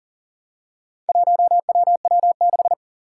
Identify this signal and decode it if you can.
Morse code, heard as a keyed tone, about 42 wpm, 700 Hz. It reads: 1WW6
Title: Four Challenging Letter-Number Combinations 40wpm